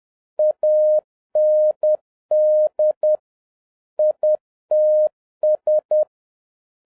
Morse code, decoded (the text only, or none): AND ITS